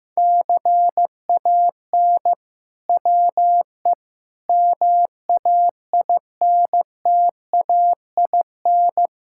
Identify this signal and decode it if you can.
Morse code, heard as a keyed tone, about 15 wpm, 700 Hz. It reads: CAN WE MAINTAIN
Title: Sentences from Top 1000 Words - Review Part 5 - 15wpm